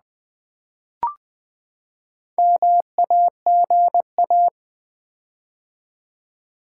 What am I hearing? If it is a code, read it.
Morse code, heard as a keyed tone, about 20 wpm, 700 Hz. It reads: MAGA
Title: Build A Word - 4 Letters At A Time - 8-Letter Words - 20wpm